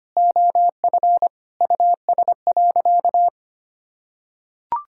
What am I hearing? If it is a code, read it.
Morse code, heard as a keyed tone, about 25 wpm, 700 Hz. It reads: OF US.
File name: practice / 25wpm / Sentences with 3-Word Phrases - 25wpm